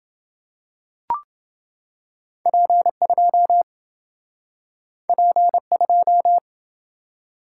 Morse code, heard as a keyed tone, about 27 wpm, 700 Hz. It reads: P2 P2